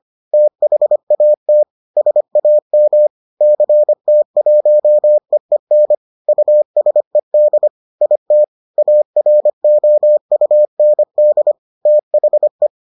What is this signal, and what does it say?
Morse code, heard as a keyed tone, about 25 wpm, 600 Hz. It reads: THAT SAM CT1EEN USED IT AROUND THE